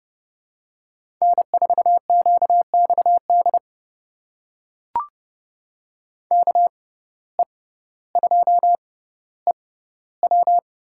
Morse code, heard as a keyed tone, about 30 wpm, spaced as for 15 wpm, 700 Hz. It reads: N4QXD KE2EW
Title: USA Call Signs - 5 Characters - 6x Character Spacing - 30wpm